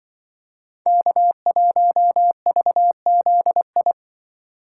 Morse code, heard as a keyed tone, about 24 wpm, 700 Hz. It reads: K1VZI
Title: Speed Racing - USA 1x3 Call Signs 20wpm